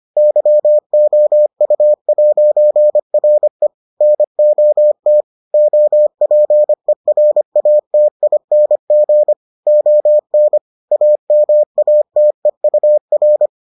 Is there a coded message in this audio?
Yes: Morse code from 0.2 to 13.5 s, heard as a keyed tone, about 25 wpm, 600 Hz. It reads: YOU'RE NOT OPERATING ON AMATEUR